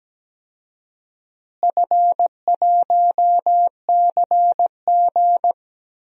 Morse code, heard as a keyed tone, about 17 wpm, 700 Hz. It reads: F1CG
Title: DXCC-100 Call Signs - 4 Characters - Repeated 2x - 17wpm